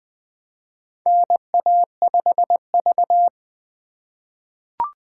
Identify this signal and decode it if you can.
Morse code, heard as a keyed tone, about 20 wpm, 700 Hz. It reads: NA5V
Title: USA Call Signs - 4 Characters - 2x Character Spacing - 20wpm